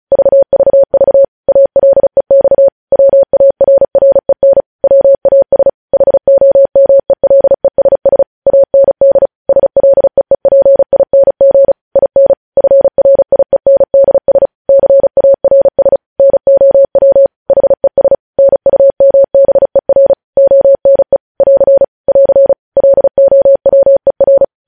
Morse code, heard as a keyed tone, about 36 wpm, 550 Hz. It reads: VVV ALEX WARREN WAS HOMELESS AND SLEEPING IN FRIENDS CARS NOW HES NUMBER ONE <AR> <AR> LOWER
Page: News headlines in Morse code at 35 wpm